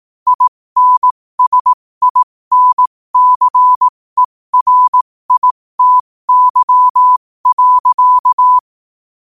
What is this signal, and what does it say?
Morse code, heard as a keyed tone, about 18 wpm, spaced as for 15 wpm, 1000 Hz. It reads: INSINCERITY.
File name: qotdfiles / 15wpm